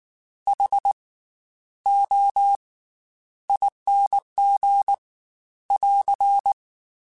Morse code, heard as a keyed tone, about 19 wpm, 800 Hz. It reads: H O ING <AR>
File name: Special INGs / Special INGs-mp3s / Slow ing @18